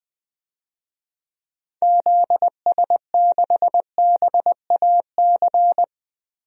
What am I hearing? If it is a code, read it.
Morse code, heard as a keyed tone, about 20 wpm, 700 Hz. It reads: ZS6BAC
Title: DXCC-100 Call Signs - 6 Characters - Repeated 3x - 20wpm